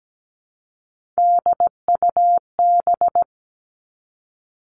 Morse code, heard as a keyed tone, about 17 wpm, 700 Hz. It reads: DUB
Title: Three Challenging Letter-Number Combinations 17wpm